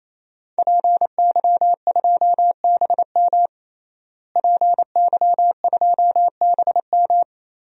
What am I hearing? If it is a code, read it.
Morse code, heard as a keyed tone, about 28 wpm, 700 Hz. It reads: PY2BM PY2BM
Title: DXCC-100 Call Signs - 5 Characters - Repeated 3x - 28wpm